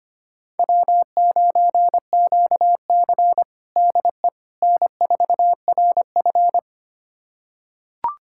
Morse code, heard as a keyed tone, about 25 wpm, 700 Hz. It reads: W9QC DE N4RF
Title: USA Callsign DE Callsign - 4 Characters - 25wpm